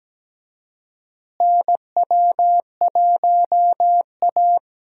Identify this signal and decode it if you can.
Morse code, heard as a keyed tone, about 17 wpm, 700 Hz. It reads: NW1A
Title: USA Call Signs - 4 Characters - Repeated 3x - 17wpm